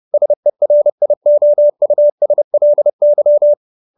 Morse code, heard as a keyed tone, about 30 wpm, 590 Hz. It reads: SERIOUSLY
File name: QSO315 30